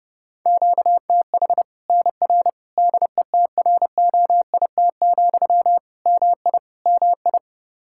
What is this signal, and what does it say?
Morse code, heard as a keyed tone, about 32 wpm, 700 Hz. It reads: QTH NR DETROIT, MI MI
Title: Practice QSO - Exchange 1 - Beginning - 30wpm